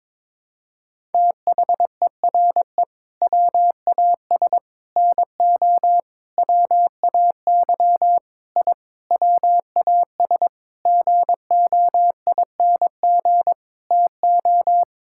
Morse code, heard as a keyed tone, about 22 wpm, 700 Hz. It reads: THERE WAS NO WAY I WAS GOING TO
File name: practice / 22wpm / Sentences with 8-Word Phrases - 22wpm